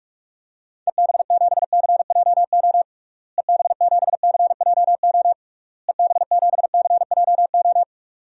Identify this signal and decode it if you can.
Morse code, heard as a keyed tone, about 41 wpm, 700 Hz. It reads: EB7CJO EB7CJO EB7CJO
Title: DXCC-100 Call Signs - 6 Characters - Repeated 3x - 45wpm